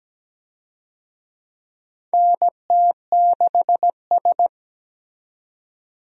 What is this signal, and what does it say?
Morse code, heard as a keyed tone, about 17 wpm, 700 Hz. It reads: NT6S
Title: USA Call Signs - 4 Characters - 17wpm